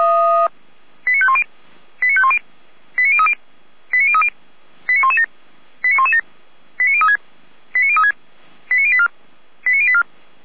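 Five-tone selective call (ZVEI), heard as quick runs of five tones, tones of about 70 ms. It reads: 89510 89510 89030 89030 89117 89117 89036 89036 89084 89084
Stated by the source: ZVEI five-tone selective call received on SDR.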